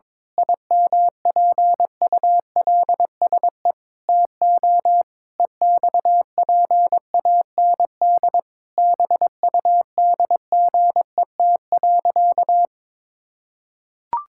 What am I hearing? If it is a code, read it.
Morse code, heard as a keyed tone, about 22 wpm, 700 Hz. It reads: IMPULSE TO EXPAND BUDGET.
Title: Pangram Challenge v2 - 22wpm